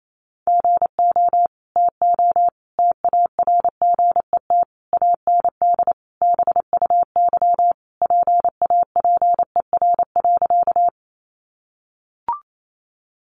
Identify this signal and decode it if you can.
Morse code, heard as a keyed tone, about 28 wpm, 700 Hz. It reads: GO TO TARGET AND BUY PAPER.
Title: Sentences from Top 1200 Words 28wpm